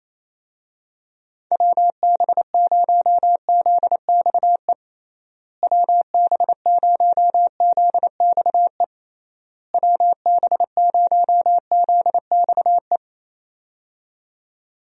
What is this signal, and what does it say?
Morse code, heard as a keyed tone, about 28 wpm, 700 Hz. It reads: WB0ZXE WB0ZXE WB0ZXE
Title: USA Call Signs - 6 Characters - Repeated 3x - 28wpm